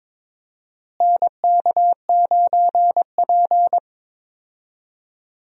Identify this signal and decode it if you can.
Morse code, heard as a keyed tone, about 21 wpm, 700 Hz. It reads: NK9P
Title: USA Call Signs - 4 Characters - Repeated 2x - 22wpm